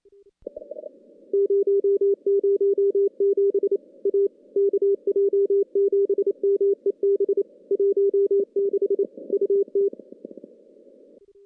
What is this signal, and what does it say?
Morse code, heard as a keyed tone, about 28 wpm, 400 Hz. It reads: 007 A KJ7MEB 16 UT
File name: contests / CW-SS-k5tr- -2300